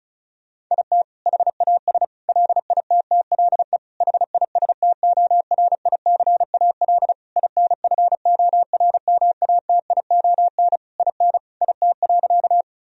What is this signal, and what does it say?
Morse code, heard as a keyed tone, about 34 wpm, 700 Hz. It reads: IT HAS LITTLE HISTORICAL INFORMATION IN IT.
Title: Sentences from Top 1000 Words - Review Part 4 - 35wpm